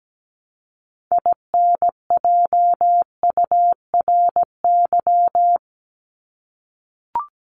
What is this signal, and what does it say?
Morse code, heard as a keyed tone, about 17 wpm, 700 Hz. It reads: INJURY
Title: Top 1751-2000 Words 17wpm